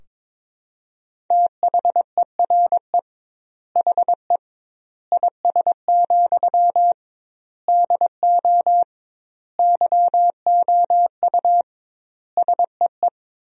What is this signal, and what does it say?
Morse code, heard as a keyed tone, about 22 wpm, 700 Hz. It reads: THERE HE IS, DO YOU SEE
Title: 2x Word-Spacing - Sentences from Top 100 Words - 22wpm